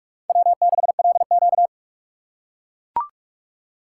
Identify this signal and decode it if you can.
Morse code, heard as a keyed tone, about 41 wpm, 700 Hz. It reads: W6LQ
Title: USA Call Signs - 4 Characters - 6x Character Spacing - 45wpm